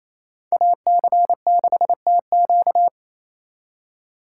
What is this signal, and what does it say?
Morse code, heard as a keyed tone, about 36 wpm, 700 Hz. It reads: AC6TQ